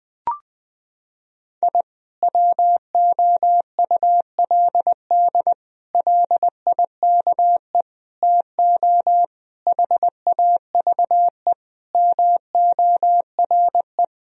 Morse code, heard as a keyed tone, about 20 wpm, 700 Hz. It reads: I WOULD LIKE TO HAVE MORE
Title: Sentences from Top 1000 Words - Review Part 3 - 20wpm